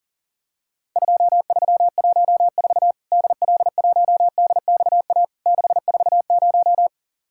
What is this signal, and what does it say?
Morse code, heard as a keyed tone, about 44 wpm, 700 Hz. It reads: 2314 DL1DXA 640
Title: Worked All Europe DX Contest - QTC Exchange - 40wpm